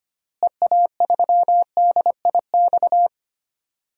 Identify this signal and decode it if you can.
Morse code, heard as a keyed tone, about 25 wpm, 700 Hz. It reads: EA3DIX
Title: DXCC-100 Call Signs - 6 Characters - Repeated 3x - 25wpm